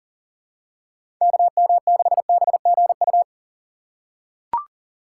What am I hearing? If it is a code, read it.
Morse code, heard as a keyed tone, about 39 wpm, 700 Hz. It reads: KM6BGU
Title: USA Call Signs - 6 Characters - 2x Character Spacing - 40wpm